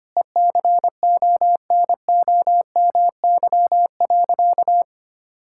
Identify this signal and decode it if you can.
Morse code, heard as a keyed tone, about 25 wpm, 700 Hz. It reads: ECONOMY.
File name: practice / 25wpm / Difficult Sentences with Hints - 25wpm